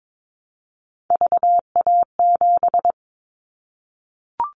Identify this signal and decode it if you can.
Morse code, heard as a keyed tone, about 22 wpm, 700 Hz. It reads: VA7
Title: Callsign Prefixes for Top 25 DXCC - Repeated 3x - 22wpm